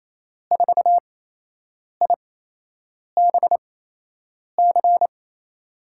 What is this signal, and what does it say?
Morse code, heard as a keyed tone, about 28 wpm, spaced as for 8 wpm, 700 Hz. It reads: 4IBC
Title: USA Call Signs - 6 Characters - 8x Character Spacing - 28wpm